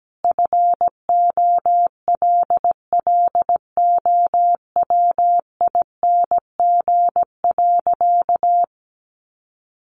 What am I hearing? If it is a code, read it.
Morse code, heard as a keyed tone, about 17 wpm, 700 Hz. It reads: FOLLOWING.
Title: Sentences from Top 1300 Words 17wpm